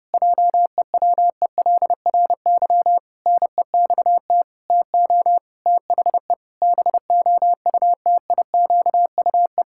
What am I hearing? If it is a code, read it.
Morse code, heard as a keyed tone, about 30 wpm, 700 Hz. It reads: JEWELRY NEXT TO THE BOUTIQUE
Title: Pangram Challenge v2 - 30wpm